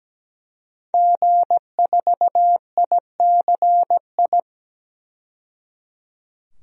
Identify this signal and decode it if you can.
Morse code, heard as a keyed tone, about 17 wpm, 700 Hz. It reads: G4ICI
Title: DXCC-100 Call Signs - 5 Characters - 17wpm